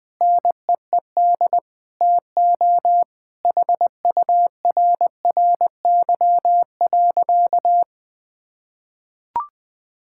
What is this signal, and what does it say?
Morse code, heard as a keyed tone, about 20 wpm, 700 Hz. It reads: NEED TO HURRY.
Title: Sentences with 5-Word Phrases - 20wpm